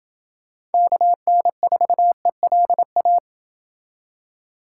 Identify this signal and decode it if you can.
Morse code, heard as a keyed tone, about 28 wpm, 700 Hz. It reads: KN4ELA